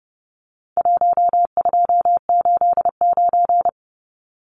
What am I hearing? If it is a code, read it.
Morse code, heard as a keyed tone, about 30 wpm, 700 Hz. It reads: 1289
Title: Numbers - Contest Style - 4-Digit 30wpm